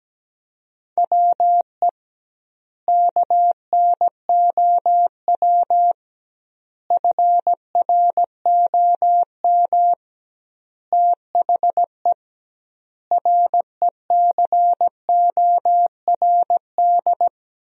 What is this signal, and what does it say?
Morse code, heard as a keyed tone, about 17 wpm, 700 Hz. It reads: WE KNOW FROM THE RECORD
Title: 2x Word-Spacing - Sentences from Top 500 Words - 17wpm